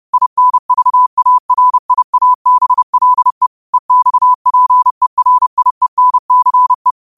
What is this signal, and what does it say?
Morse code, heard as a keyed tone, about 30 wpm, 1000 Hz. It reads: INVARIABLE EXPERIENCE
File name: qotdfiles / 30wpm